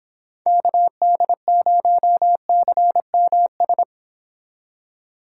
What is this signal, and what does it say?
Morse code, heard as a keyed tone, about 26 wpm, 700 Hz. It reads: KD0CMS